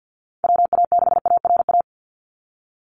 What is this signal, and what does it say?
Morse code, heard as a keyed tone, about 50 wpm, 700 Hz. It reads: WA6ARA
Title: NA SOTA Chaser Callsigns 50wpm